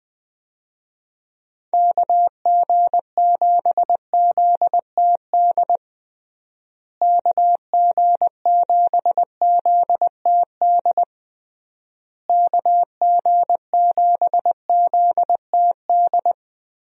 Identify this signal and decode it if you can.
Morse code, heard as a keyed tone, about 20 wpm, 700 Hz. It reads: KG7ZTD KG7ZTD KG7ZTD
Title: USA Call Signs - 6 Characters - Repeated 3x - 20wpm